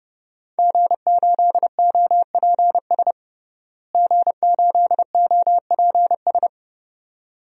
Morse code, heard as a keyed tone, about 29 wpm, 700 Hz. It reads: G8OPS G8OPS
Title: DXCC-100 Call Signs - 5 Characters - Repeated 2x - 30wpm